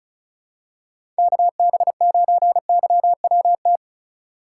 Morse code, heard as a keyed tone, about 36 wpm, 700 Hz. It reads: KB9YWT
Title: USA Call Signs - 6 Characters - 2x Character Spacing - 35wpm